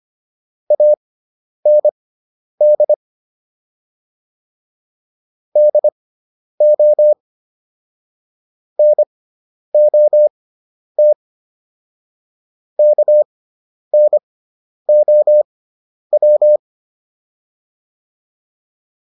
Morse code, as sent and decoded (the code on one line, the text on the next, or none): .- -. -.. / -.. --- / -. --- - / -.- -. --- .--
AND DO NOT KNOW